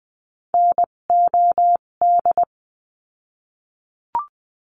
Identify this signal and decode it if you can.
Morse code, heard as a keyed tone, about 20 wpm, spaced as for 17 wpm, 700 Hz. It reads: NOD
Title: 257 Lesson 42b - Words 14-20wpm